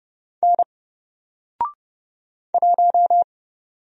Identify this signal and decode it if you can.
Morse code, heard as a keyed tone, about 30 wpm, 700 Hz. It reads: N 1
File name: Single-Letter-Number-30wpm